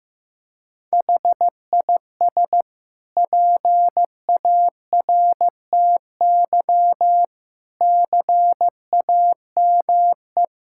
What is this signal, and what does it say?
Morse code, heard as a keyed tone, about 15 wpm, 700 Hz. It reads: HIS PARTY CAME
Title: Sentences from Top 1600 Words 15wpm